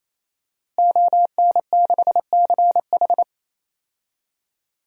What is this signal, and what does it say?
Morse code, heard as a keyed tone, about 28 wpm, 700 Hz. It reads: ON6CH
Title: DXCC-100 Call Signs - 5 Characters - Repeated 3x - 28wpm